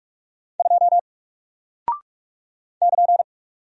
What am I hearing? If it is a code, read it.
Morse code, heard as a keyed tone, about 45 wpm, 700 Hz. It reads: 2 <KN>
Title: Single Character (Letter:Number:Punctuation:Prosign) Alt 45wpm